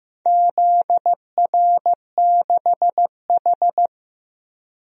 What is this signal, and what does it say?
Morse code, heard as a keyed tone, about 15 wpm, 700 Hz. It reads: ZR6H
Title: DXCC-100 Call Signs - 4 Characters - 15wpm